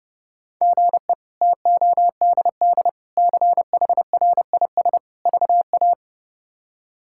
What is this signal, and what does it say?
Morse code, heard as a keyed tone, about 30 wpm, 700 Hz. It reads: GE TODD CHRIS VA
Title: K1USN SST QSO Practice - 30wpm